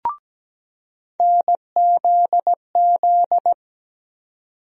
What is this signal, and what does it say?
Morse code, heard as a keyed tone, about 17 wpm, 700 Hz. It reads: NZZ
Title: USA Call Sign Postfixes - 3 Letters 17wpm